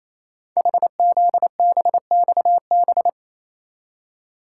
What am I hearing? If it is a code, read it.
Morse code, heard as a keyed tone, about 28 wpm, 700 Hz. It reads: HZBXB
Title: Code Groups - 5 Letters - 28wpm